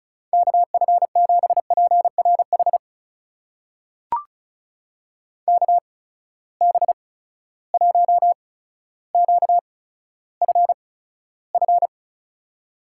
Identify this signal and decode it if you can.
Morse code, heard as a keyed tone, about 36 wpm, spaced as for 15 wpm, 700 Hz. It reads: KF7PRH KB1QFF
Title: USA Call Signs - 6 Characters - 8x Character Spacing - 35wpm